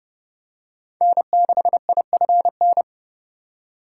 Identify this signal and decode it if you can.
Morse code, heard as a keyed tone, about 30 wpm, 700 Hz. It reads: N6IFN
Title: USA Call Signs - 2x Character Spacing - 30wpm